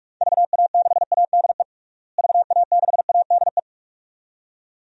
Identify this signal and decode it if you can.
Morse code, heard as a keyed tone, about 42 wpm, 700 Hz. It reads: VA6ADE VA6ADE
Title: DXCC-100 Call Signs - 6 Characters - Repeated 2x - 45wpm